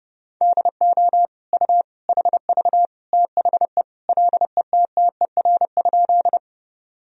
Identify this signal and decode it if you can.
Morse code, heard as a keyed tone, about 30 wpm, 700 Hz. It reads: DO U HV THE LETTER?
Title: Sentences from Top 300 Words and Abbreviations 30wpm